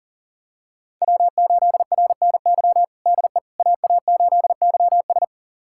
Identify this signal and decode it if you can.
Morse code, heard as a keyed tone, about 40 wpm, 700 Hz. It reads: W8RNY DE AA8YS
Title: USA Callsign DE Callsign 40wpm